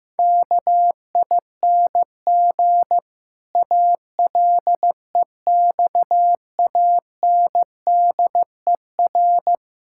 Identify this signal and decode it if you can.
Morse code, heard as a keyed tone, about 15 wpm, 700 Hz. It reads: KING ALEXANDER
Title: Pangram Challenge v2 - 15wpm